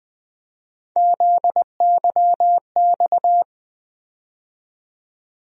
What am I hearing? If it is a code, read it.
Morse code, heard as a keyed tone, about 20 wpm, 700 Hz. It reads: ZYX
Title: USA Call Sign Postfixes - 3 Letters 20wpm